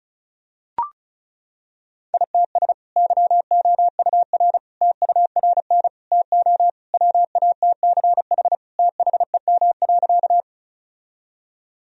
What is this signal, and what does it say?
Morse code, heard as a keyed tone, about 36 wpm, 700 Hz. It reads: ITS YOUR TURN TO WATCH THEM.
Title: Sentences from Top 1000 Words - Review Part 1 - 35wpm